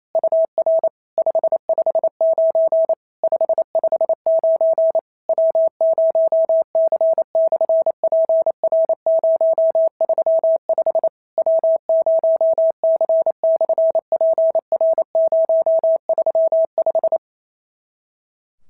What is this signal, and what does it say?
Morse code, heard as a keyed tone, about 28 wpm, 650 Hz. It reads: UR 559 559 W0C/PR035 W0C/PR035
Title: SOTA QSO Practice 28wpm